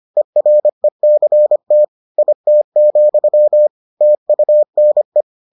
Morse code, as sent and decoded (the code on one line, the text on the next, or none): . .-. -.-. - / .. - --..-- / - ..- -. .
ERECT IT, TUNE